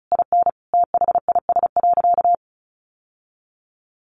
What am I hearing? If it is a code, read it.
Morse code, heard as a keyed tone, about 34 wpm, 700 Hz. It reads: IN THIS.
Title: Sentences from Top 1500 Words 35wpm